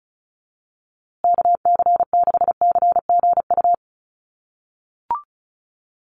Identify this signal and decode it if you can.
Morse code, heard as a keyed tone, about 34 wpm, 700 Hz. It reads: KC6CGU
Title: USA Call Signs Repeated 2x 35wpm